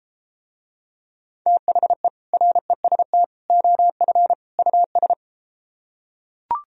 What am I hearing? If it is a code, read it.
Morse code, heard as a keyed tone, about 33 wpm, 700 Hz. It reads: THE REST OF US